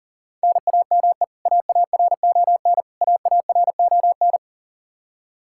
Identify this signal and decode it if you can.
Morse code, heard as a keyed tone, about 40 wpm, 700 Hz. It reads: NAME AARON AARON
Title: Practice QSO - Exchange 1 - Beginning - 40wpm